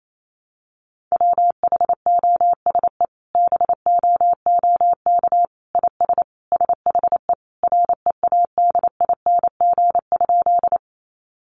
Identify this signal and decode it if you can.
Morse code, heard as a keyed tone, about 28 wpm, 700 Hz. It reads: WHOSE BOOK IS SHE READING?
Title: Sentences from Top 1200 Words 28wpm